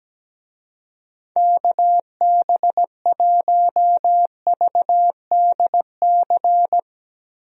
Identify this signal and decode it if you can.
Morse code, heard as a keyed tone, about 17 wpm, 700 Hz. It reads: KB1VDC
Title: USA Call Signs - 6 Characters - Repeated 3x - 17wpm